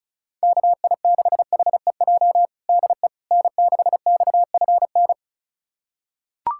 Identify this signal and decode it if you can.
Morse code, heard as a keyed tone, about 36 wpm, 690 Hz. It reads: KI6HEJ DE N6XFN
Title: USA Callsign DE Callsign 35wpm